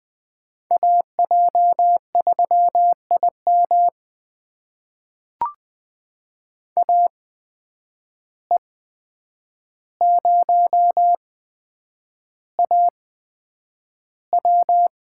Morse code, heard as a keyed tone, about 20 wpm, spaced as for 9 wpm, 700 Hz. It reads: AJ3IM AE0AW
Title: USA Call Signs - 5 Characters - 8x Character Spacing - 20wpm